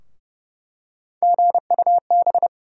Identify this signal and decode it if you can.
Morse code, heard as a keyed tone, about 32 wpm, 700 Hz. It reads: GUB
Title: USA Call Sign Postfixes - 3 Letters 30wpm